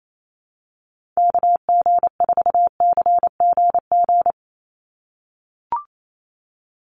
Morse code, heard as a keyed tone, about 28 wpm, 700 Hz. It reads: KG4CGG
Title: USA Call Signs 28wpm